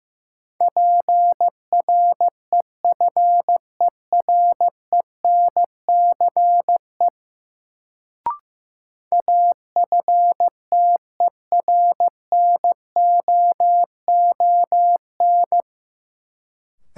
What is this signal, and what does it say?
Morse code, heard as a keyed tone, about 15 wpm, 700 Hz. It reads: PREFERENCE AFTERNOON
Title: Sentences from Top 1500 Words - New Word Drill - 15wpm